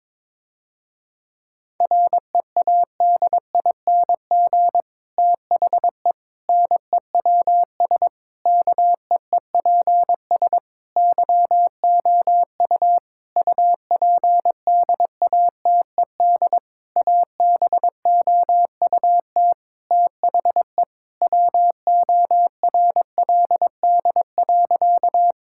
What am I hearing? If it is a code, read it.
Morse code, heard as a keyed tone, about 22 wpm, 700 Hz. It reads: READING THE NEWS KEEPS YOU UPDATED ABOUT THE WORLD.
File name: Difficult Sentences with Hints - 22wpm